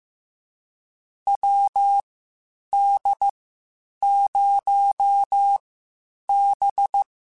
Morse code, heard as a keyed tone, about 15 wpm, spaced as for 8 wpm, 800 Hz. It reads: WD0B
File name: QSOs / QSOs-mp3s / WD0BGP 1@10